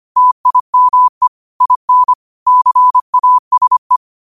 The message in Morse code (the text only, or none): TIME IN CASE